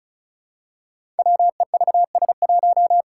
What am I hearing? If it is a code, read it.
Morse code, heard as a keyed tone, about 36 wpm, 700 Hz. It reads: WEVS1
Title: Five Challenging Letter-Number Combinations 35wpm